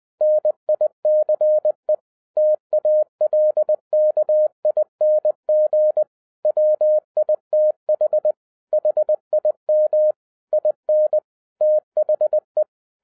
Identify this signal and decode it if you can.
Morse code, heard as a keyed tone, about 20 wpm, 610 Hz. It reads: NICE TALKING WITH HIM IN THE